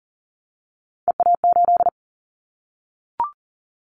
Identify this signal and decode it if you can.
Morse code, heard as a keyed tone, about 40 wpm, 700 Hz. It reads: EA8
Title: Callsign Prefixes for Top 50 DXCC - 40wpm